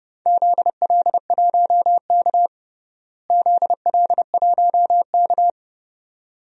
Morse code, heard as a keyed tone, about 30 wpm, 700 Hz. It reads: ZL1K ZL1K
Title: DXCC-100 Call Signs - 4 Characters - Repeated 2x - 30wpm